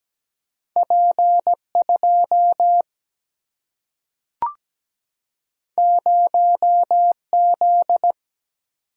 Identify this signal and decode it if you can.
Morse code, heard as a keyed tone, about 17 wpm, 700 Hz. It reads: P2 0Z
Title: Single-Letter Single-Number 17wpm